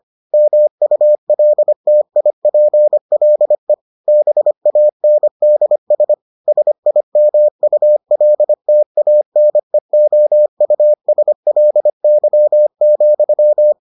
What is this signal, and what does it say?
Morse code, heard as a keyed tone, about 25 wpm, 600 Hz. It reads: MULTIPLE BANDS SIMULTANEOUSLY,